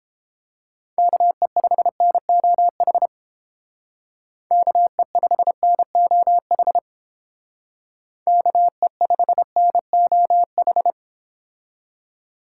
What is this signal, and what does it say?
Morse code, heard as a keyed tone, about 32 wpm, 700 Hz. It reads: KE5NOH KE5NOH KE5NOH